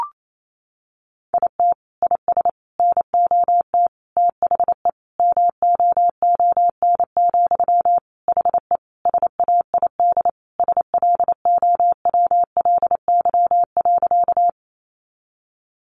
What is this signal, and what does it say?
Morse code, heard as a keyed tone, about 28 wpm, 700 Hz. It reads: IT IS NOT THE MOON, HE SAID SLOWLY.
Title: Sentences from Top 1100 Words 28wpm